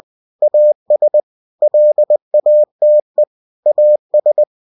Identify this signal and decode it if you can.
Morse code, heard as a keyed tone, about 20 wpm, 600 Hz. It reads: AS LATE AS